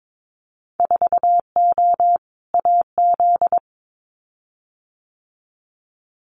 Morse code, heard as a keyed tone, about 22 wpm, 700 Hz. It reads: 4O AZ
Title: Winter Field Day Exchanges Encore 22wpm